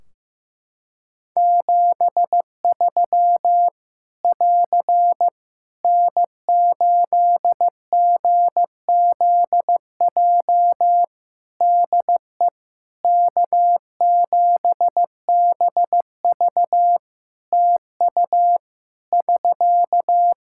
Morse code, heard as a keyed tone, about 15 wpm, 700 Hz. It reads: 73 <AR> N8GZJ DE K7BV TU <SK>
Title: Practice QSO - Exchange 4 - Ending - 15wpm